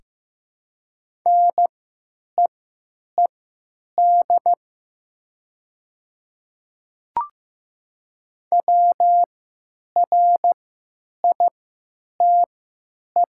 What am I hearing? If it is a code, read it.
Morse code, heard as a keyed tone, about 15 wpm, spaced as for 9 wpm, 700 Hz. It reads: NEED WRITE
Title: 3x Character Spacing - Top 200 Words - 15wpm